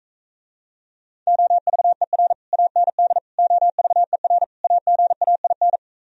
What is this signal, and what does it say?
Morse code, heard as a keyed tone, about 45 wpm, 700 Hz. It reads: OVER AND OVER AGAIN